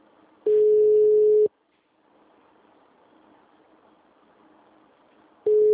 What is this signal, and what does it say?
Ringback tone (Europe, 425 Hz).